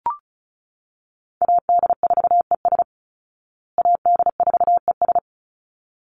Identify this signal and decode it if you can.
Morse code, heard as a keyed tone, about 34 wpm, 700 Hz. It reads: AD4ES AD4ES
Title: Call Signs Repeated 3x - 35wpm